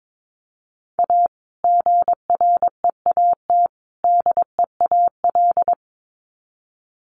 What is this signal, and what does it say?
Morse code, heard as a keyed tone, about 22 wpm, 700 Hz. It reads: A GREAT DEAL
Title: Sets of 3 Words 22wpm